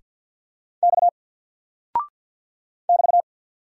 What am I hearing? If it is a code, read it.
Morse code, heard as a keyed tone, about 50 wpm, 700 Hz. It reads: X <BT>
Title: Single Character (Letter:Number:Punctuation:Prosign) 50wpm